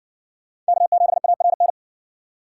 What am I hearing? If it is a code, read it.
Morse code, heard as a keyed tone, about 60 wpm, 700 Hz. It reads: K7ARN